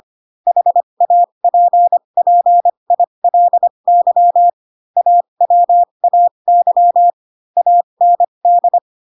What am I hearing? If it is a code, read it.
Morse code, heard as a keyed tone, about 25 wpm, spaced as for 22 wpm, 700 Hz. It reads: HAPPILY AWAY AND
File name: Pinocchio / Pinocchio ch3 18 25 ik1hge